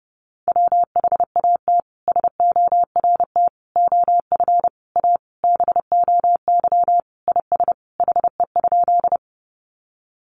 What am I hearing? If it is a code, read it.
Morse code, heard as a keyed tone, about 28 wpm, 700 Hz. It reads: WHAT SORT OF A BOY IS HE?
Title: Sentences from Top 1300 Words 30wpm